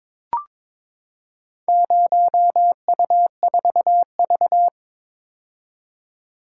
Morse code, heard as a keyed tone, about 22 wpm, 700 Hz. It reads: 0U4V
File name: Four Letter-Number Combinations - 22wpm